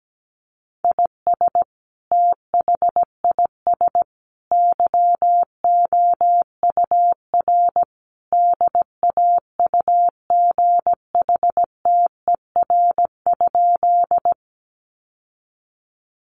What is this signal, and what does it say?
Morse code, heard as a keyed tone, about 17 wpm, 700 Hz. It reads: IS THIS YOUR DAUGHTER?
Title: Sentences from Top 1500 Words 17wpm